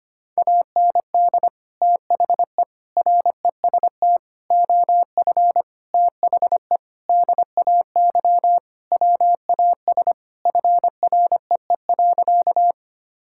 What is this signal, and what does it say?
Morse code, heard as a keyed tone, about 25 wpm, 700 Hz. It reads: AND THE REST OF THE DAY WAS FREE.
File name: practice / 25wpm / Sentences with 4-Word Phrases - 25wpm